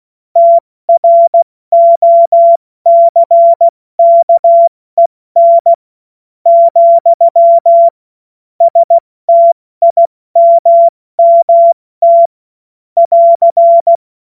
Morse code, heard as a keyed tone, about 16 wpm, spaced as for 13 wpm, 680 Hz. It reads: TROCKEN, STIMMT <AR>
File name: Morsekurs / Witze / Witz 15. Februar 2022 70 F80